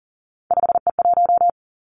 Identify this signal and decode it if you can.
Morse code, heard as a keyed tone, about 38 wpm, 700 Hz. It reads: 5E1
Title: Three Challenging Letter-Number Combinations 40wpm